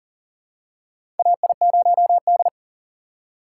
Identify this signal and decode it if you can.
Morse code, heard as a keyed tone, about 40 wpm, 700 Hz. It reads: AI0D
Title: USA Call Signs - 4 Characters - 6x Character Spacing - 40wpm